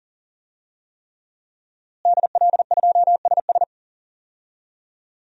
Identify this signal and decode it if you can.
Morse code, heard as a keyed tone, about 40 wpm, 700 Hz. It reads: DL2SS
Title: DXCC-100 Call Signs - 5 Characters - Repeated 3x - 40wpm